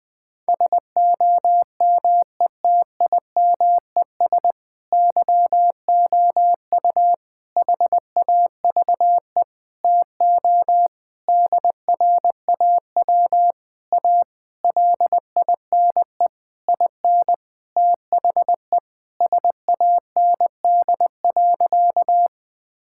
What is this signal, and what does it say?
Morse code, heard as a keyed tone, about 21 wpm, 700 Hz. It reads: SOMETIMES YOU HAVE TO DRAW A LINE IN THE SAND.
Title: Sentences with 7-Word Phrases - Spoken Beginning - 20wpm